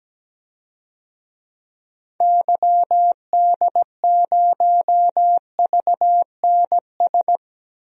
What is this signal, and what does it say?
Morse code, heard as a keyed tone, about 17 wpm, 700 Hz. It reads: YD0VNS
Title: DXCC-100 Call Signs - 6 Characters - 17wpm